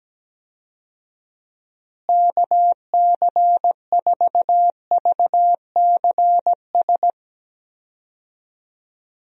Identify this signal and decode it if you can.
Morse code, heard as a keyed tone, about 17 wpm, 700 Hz. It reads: KC4VCS
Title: USA Call Signs - 6 Characters - Repeated 2x - 17wpm